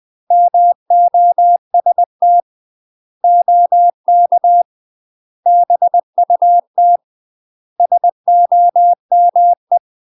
Morse code, heard as a keyed tone, about 20 wpm, 700 Hz. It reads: MOST OK BUT SOME
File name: UK MASTER QSOs TRAINING / 20WPM / UK MASTER QSOs 0050